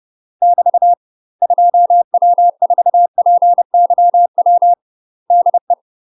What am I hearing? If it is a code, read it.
Morse code, heard as a keyed tone, about 30 wpm, 700 Hz. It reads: <BT> 2W4PYW DE
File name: UK MASTER QSOs TRAINING / 30WPM / UK MASTER QSOs 0036